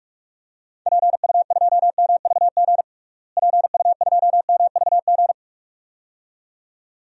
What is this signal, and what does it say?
Morse code, heard as a keyed tone, about 46 wpm, 700 Hz. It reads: PU2MVG PU2MVG
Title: DXCC-100 Call Signs - 6 Characters - Repeated 2x - 45wpm